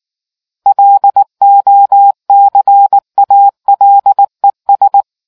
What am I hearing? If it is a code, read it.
Morse code, heard as a keyed tone, about 19 wpm, 800 Hz. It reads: LOCALES